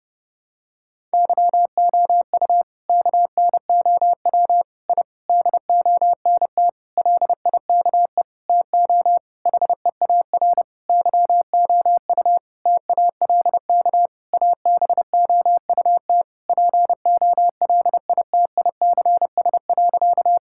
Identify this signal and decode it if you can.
Morse code, heard as a keyed tone, about 32 wpm, 700 Hz. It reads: YOU KNOW I DONT LIKE TO HEAR YOU TALK ABOUT POLITICS.
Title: Sentences from Top 1000 Words - Review Part 6 - 30wpm